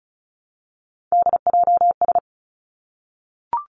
Morse code, heard as a keyed tone, about 36 wpm, 700 Hz. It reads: DJS
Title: Three Challenging Letter-Number Combinations 35wpm